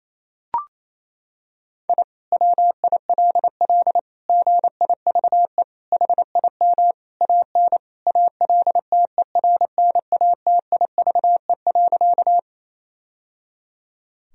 Morse code, heard as a keyed tone, about 28 wpm, 700 Hz. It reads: I WILL GIVE HIM AN ALTERNATIVE.
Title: Sentences from Top 1000 Words - Review Part 4 - 28wpm